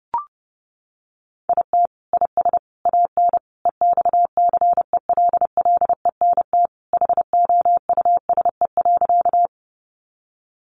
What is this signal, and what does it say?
Morse code, heard as a keyed tone, about 30 wpm, 700 Hz. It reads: IT IS AN EXCELLENT HOUSE.
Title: Sentences from Top 1000 Words 30wpm